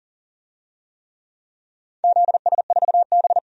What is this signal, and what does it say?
Morse code, heard as a keyed tone, about 40 wpm, 700 Hz. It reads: ZS4B
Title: DXCC-100 Call Signs - 4 Characters - 40wpm